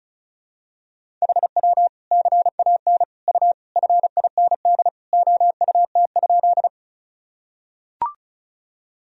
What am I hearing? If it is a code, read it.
Morse code, heard as a keyed tone, about 34 wpm, 700 Hz. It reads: HW CAN U FIND OUT?
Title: Sentences from Top 200 Words and Abbreviations 35wpm